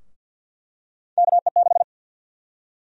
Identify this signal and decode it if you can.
Morse code, heard as a keyed tone, about 52 wpm, 700 Hz. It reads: KE6
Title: USA Call Sign Prefixes - 50wpm